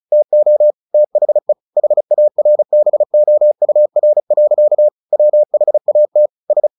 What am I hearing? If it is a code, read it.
Morse code, heard as a keyed tone, about 36 wpm, 600 Hz. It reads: TO THE HARBOUR. WHAT S
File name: the old man and the sea 2 35wpm